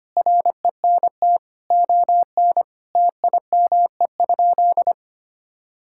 Morse code, heard as a keyed tone, about 25 wpm, 700 Hz. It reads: RENT ON TIME?
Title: Sentences from Top 1000 Words - Review Part 5 - 25wpm